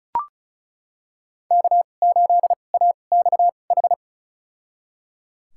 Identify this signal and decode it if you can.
Morse code, heard as a keyed tone, about 36 wpm, spaced as for 25 wpm, 700 Hz. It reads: K8AXH
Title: USA Call Signs - 5 Characters - 2x Character Spacing - 35wpm